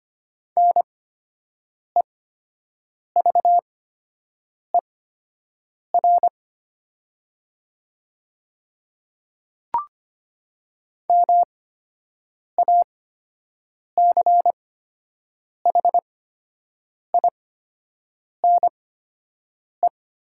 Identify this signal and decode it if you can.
Morse code, heard as a keyed tone, about 25 wpm, spaced as for 7 wpm, 700 Hz. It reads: NEVER MACHINE
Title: 8x Character Spacing - Top 500 Words - 25wpm